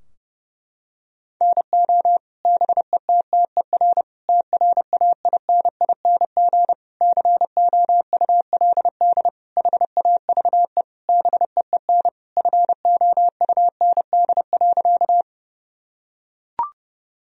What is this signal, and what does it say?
Morse code, heard as a keyed tone, about 30 wpm, 700 Hz. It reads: NO BETTER TRAINING COULD HAVE BEEN FOUND.
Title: Sentences from Top 1000 Words - Review Part 4 - 30wpm